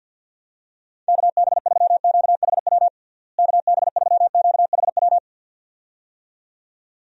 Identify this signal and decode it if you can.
Morse code, heard as a keyed tone, about 46 wpm, 700 Hz. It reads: KB3QHW KB3QHW
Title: USA Call Signs - 6 Characters - Repeated 2x - 50wpm